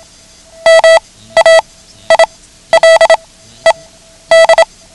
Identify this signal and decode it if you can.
Morse code, heard as a keyed tone, about 27 wpm, spaced as for 13 wpm, 650 Hz. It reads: MAILED
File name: ham radio / CodeQuick / Code Quick 09